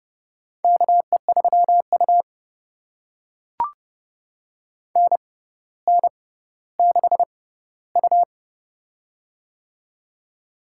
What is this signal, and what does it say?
Morse code, heard as a keyed tone, about 30 wpm, spaced as for 15 wpm, 700 Hz. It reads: KE3U NN6U
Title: USA Call Signs - 4 Characters - 6x Character Spacing - 30wpm